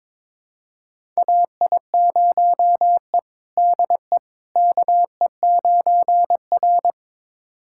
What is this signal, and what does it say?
Morse code, heard as a keyed tone, about 22 wpm, 700 Hz. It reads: AI0E DE KE9R
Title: USA Callsign DE Callsign - 4 Characters - 22wpm